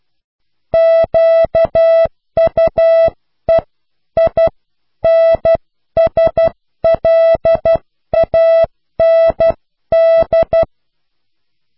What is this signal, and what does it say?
Morse code, heard as a keyed tone, about 12 wpm, spaced as for 10 wpm, 660 Hz. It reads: QUEINSLAND
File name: cw / pseudo worden(10)